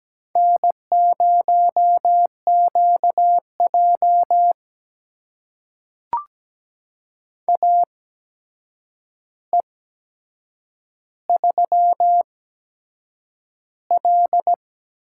Morse code, heard as a keyed tone, about 17 wpm, spaced as for 7 wpm, 700 Hz. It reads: N0QJ AE3L
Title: USA Call Signs - 4 Characters - 8x Character Spacing - 17wpm